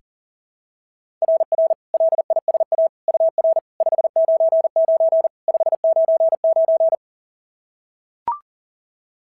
Morse code, heard as a keyed tone, about 43 wpm, 690 Hz. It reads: RR LISA UR 599 599
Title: SOTA QSO Practice 40wpm